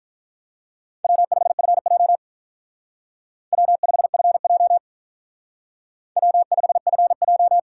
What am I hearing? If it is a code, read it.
Morse code, heard as a keyed tone, about 43 wpm, 700 Hz. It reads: W5FJ W5FJ W5FJ
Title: Speed Racing - USA 1x2 Call Signs 35wpm